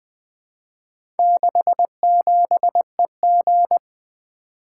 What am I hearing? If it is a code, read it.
Morse code, heard as a keyed tone, about 20 wpm, 700 Hz. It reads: 67EG
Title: Four Letter-Number Combinations - 20wpm